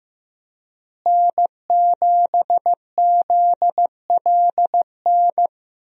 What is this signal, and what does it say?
Morse code, heard as a keyed tone, about 15 wpm, 700 Hz. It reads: N7ZLN
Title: USA Call Signs - 5 Characters - Repeated 3x - 15wpm